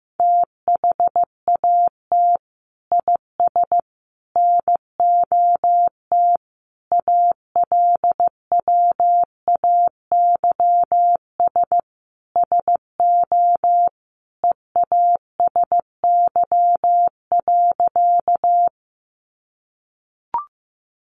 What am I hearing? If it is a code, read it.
Morse code, heard as a keyed tone, about 15 wpm, 700 Hz. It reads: THAT IS NOT ALWAYS SO EASY.
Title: Sentences from Top 700 Words 15wpm